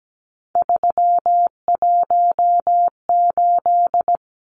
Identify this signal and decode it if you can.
Morse code, heard as a keyed tone, about 17 wpm, 700 Hz. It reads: 318
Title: Numbers - 3-Digit 17wpm